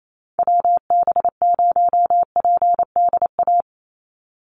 Morse code, heard as a keyed tone, about 28 wpm, 700 Hz. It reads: WB0PDA
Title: USA Call Signs 28wpm